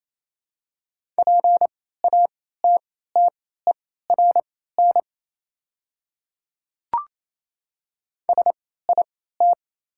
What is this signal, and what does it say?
Morse code, heard as a keyed tone, about 28 wpm, spaced as for 16 wpm, 700 Hz. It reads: PATTERN SIT
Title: 3x Character Spacing - Top 500 Words - 28wpm